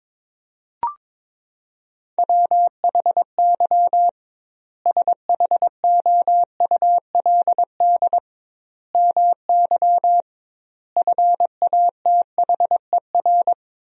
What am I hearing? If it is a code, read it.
Morse code, heard as a keyed tone, about 22 wpm, 700 Hz. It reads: WHY SHOULD MY FATHER
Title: 2x Word-Spacing - Sentences from Top 500 Words - 22wpm